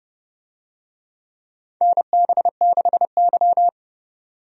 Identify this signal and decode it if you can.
Morse code, heard as a keyed tone, about 30 wpm, 700 Hz. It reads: NB6Y
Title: USA Call Signs - 4 Characters - 30wpm